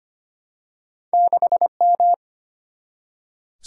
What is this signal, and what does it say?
Morse code, heard as a keyed tone, about 25 wpm, 700 Hz. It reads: 6M
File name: Two Letter-Number Combinations - 25wpm